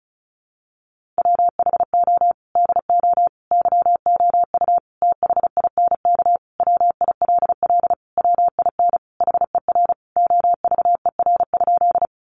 Morse code, heard as a keyed tone, about 34 wpm, 700 Hz. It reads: WHO DO YOU THINK WILL WIN HER OVER?
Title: Sentences from Top 1500 Words 35wpm